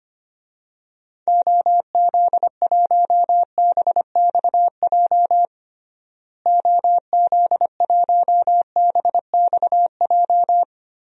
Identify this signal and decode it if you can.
Morse code, heard as a keyed tone, about 25 wpm, 700 Hz. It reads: OZ1BXJ OZ1BXJ
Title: DXCC-100 Call Signs - 6 Characters - Repeated 3x - 25wpm